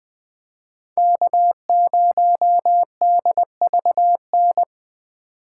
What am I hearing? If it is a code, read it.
Morse code, heard as a keyed tone, about 20 wpm, 700 Hz. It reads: K0DVN
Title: Speed Racing - USA Call Signs 17wpm